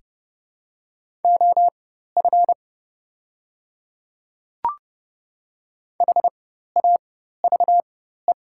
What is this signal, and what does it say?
Morse code, heard as a keyed tone, about 30 wpm, spaced as for 14 wpm, 690 Hz. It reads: OF HAVE
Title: 4x Character Spacing - Top 100 Words - 30wpm